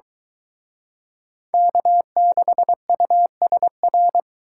Morse code, heard as a keyed tone, about 23 wpm, 700 Hz. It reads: K6USR